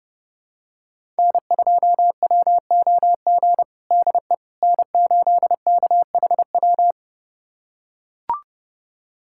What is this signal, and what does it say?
Morse code, heard as a keyed tone, about 30 wpm, 700 Hz. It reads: N2WOG DE N8KHW
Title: USA Callsign DE Callsign - 5 Characters - 30wpm